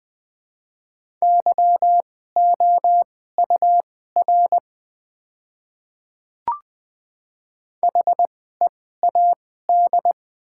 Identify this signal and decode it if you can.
Morse code, heard as a keyed tone, about 20 wpm, spaced as for 14 wpm, 700 Hz. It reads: YOUR HEAD
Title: 2x Character Spacing - Top 300 Words - 20wpm